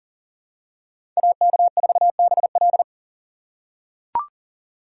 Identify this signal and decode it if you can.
Morse code, heard as a keyed tone, about 40 wpm, 700 Hz. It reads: AK4BL
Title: USA Call Signs - 5 Characters - 3x Character Spacing - 40wpm